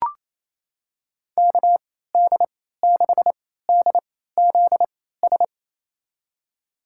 Morse code, heard as a keyed tone, about 28 wpm, spaced as for 16 wpm, 700 Hz. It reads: KD6DZS
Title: USA Call Signs - 3x Character Spacing - 28wpm